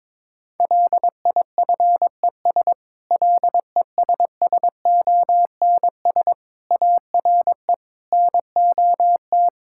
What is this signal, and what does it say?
Morse code, heard as a keyed tone, about 22 wpm, 700 Hz. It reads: LIFES LESSONS ARE NOT
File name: Difficult Sentences with Hints - 22wpm